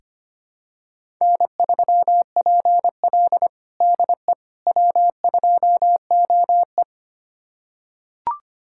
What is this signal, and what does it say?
Morse code, heard as a keyed tone, about 25 wpm, 700 Hz. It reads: N3PL DE W2OE
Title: USA Callsign DE Callsign - 4 Characters - 25wpm